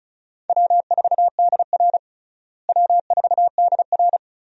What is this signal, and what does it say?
Morse code, heard as a keyed tone, about 34 wpm, 700 Hz. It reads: W4DR W4DR
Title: USA Call Signs - 4 Characters - Repeated 2x - 35wpm